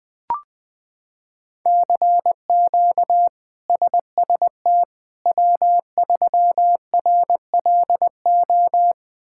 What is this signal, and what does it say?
Morse code, heard as a keyed tone, about 20 wpm, 700 Hz. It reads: CQ SST W3RLO
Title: K1USN SST QSO Practice - 20wpm